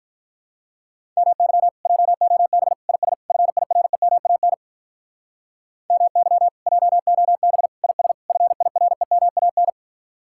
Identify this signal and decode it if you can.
Morse code, heard as a keyed tone, about 46 wpm, 700 Hz. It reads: MY JOB IS FIREMAN MY JOB IS FIREMAN